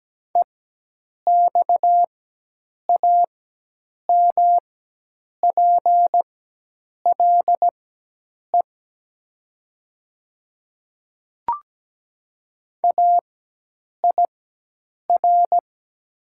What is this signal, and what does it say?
Morse code, heard as a keyed tone, about 17 wpm, spaced as for 8 wpm, 700 Hz. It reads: EXAMPLE AIR
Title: 4x Character Spacing - Top 500 Words - 17wpm